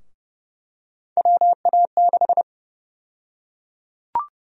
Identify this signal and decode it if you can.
Morse code, heard as a keyed tone, about 30 wpm, 700 Hz. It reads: WA6
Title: USA Call Sign Prefixes - 30wpm